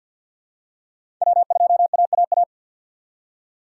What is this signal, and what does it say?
Morse code, heard as a keyed tone, about 53 wpm, 700 Hz. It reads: W2AAA
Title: USA Call Signs - 5 Characters - 4x Character Spacing - 50wpm